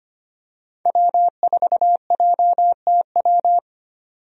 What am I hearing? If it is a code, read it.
Morse code, heard as a keyed tone, about 25 wpm, 690 Hz. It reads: W4JTW
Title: USA Call Signs - 5 Characters - 25wpm